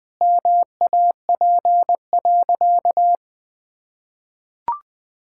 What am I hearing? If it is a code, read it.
Morse code, heard as a keyed tone, about 20 wpm, 700 Hz. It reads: MAP.
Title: Sentences with 5-Word Phrases - 20wpm